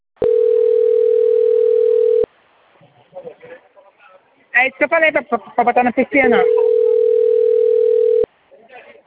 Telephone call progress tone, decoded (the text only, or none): ringback tone